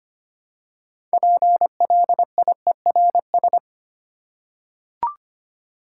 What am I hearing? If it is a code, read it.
Morse code, heard as a keyed tone, about 25 wpm, 700 Hz. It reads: PLIERS
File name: practice / 25wpm / Difficult Words with Hints - 25wpm